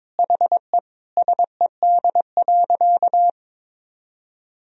Morse code, heard as a keyed tone, about 22 wpm, 700 Hz. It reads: HE SED.
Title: Sentences from Top 500 Words and Abbreviations 22wpm